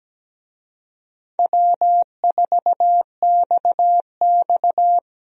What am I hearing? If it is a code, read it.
Morse code, heard as a keyed tone, about 17 wpm, 700 Hz. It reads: W4XX
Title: USA Call Signs - 4 Characters - 17wpm